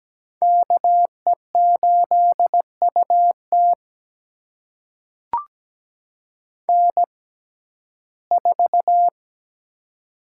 Morse code, heard as a keyed tone, about 17 wpm, spaced as for 12 wpm, 700 Hz. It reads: KE8UT N4
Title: USA Call Signs - 5 Characters - 6x Character Spacing - 17wpm